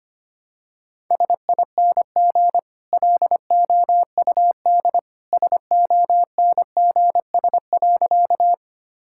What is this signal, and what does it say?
Morse code, heard as a keyed tone, about 25 wpm, 690 Hz. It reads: SING LOUD SONGS.
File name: Easy Sentences with Hints - 25wpm